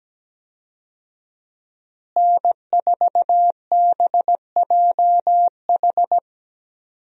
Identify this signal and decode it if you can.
Morse code, heard as a keyed tone, about 17 wpm, 700 Hz. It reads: N4BJH
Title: USA Call Signs - 5 Characters - Repeated 3x - 17wpm